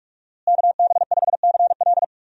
Morse code, heard as a keyed tone, about 45 wpm, 700 Hz. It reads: KB5CL
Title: USA Call Signs - 5 Characters - Repeated 3x - 45wpm